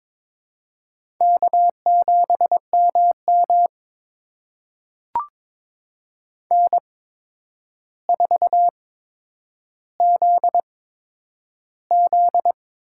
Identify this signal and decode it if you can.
Morse code, heard as a keyed tone, about 22 wpm, spaced as for 9 wpm, 700 Hz. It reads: K7MM N4ZZ
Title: USA Call Signs - 4 Characters - 8x Character Spacing - 22wpm